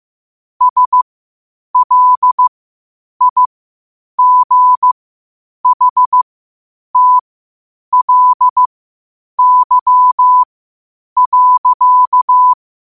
Morse code, heard as a keyed tone, about 15 wpm, spaced as for 8 wpm, 1000 Hz. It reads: SLIGHTLY.
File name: qotdfiles / 8wpm